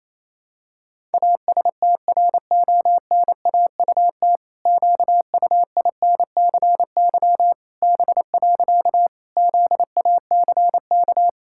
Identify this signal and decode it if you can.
Morse code, heard as a keyed tone, about 28 wpm, 700 Hz. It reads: ASTRONAUT QUINCY B. ZACK
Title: Pangram Challenge v2 - 28wpm